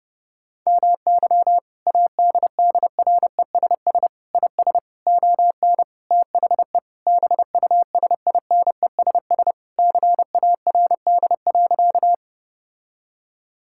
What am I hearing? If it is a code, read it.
Morse code, heard as a keyed tone, about 28 wpm, 700 Hz. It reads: MY ADDRESS IS ON THE BUSINESS CARD.
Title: Sentences from Top 1000 Words - Review Part 6 - 30wpm